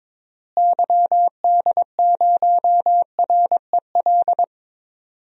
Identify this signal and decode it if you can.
Morse code, heard as a keyed tone, about 22 wpm, 700 Hz. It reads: YD0REL
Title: DXCC-100 Call Signs - 6 Characters - Repeated 2x - 22wpm